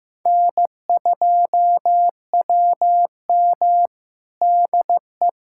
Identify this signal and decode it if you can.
Morse code, heard as a keyed tone, about 16 wpm, 700 Hz. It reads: N2WM DE
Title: USA Callsign DE Callsign - 4 Characters - 15wpm